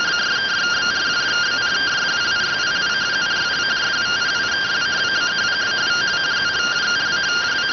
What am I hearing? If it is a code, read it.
Two-tone teleprinter signal, about 170 Hz shift.